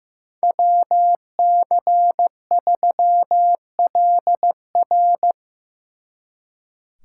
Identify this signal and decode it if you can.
Morse code, heard as a keyed tone, about 17 wpm, 700 Hz. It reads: WC3LR
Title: Speed Racing - USA 2x2 Call Signs 15wpm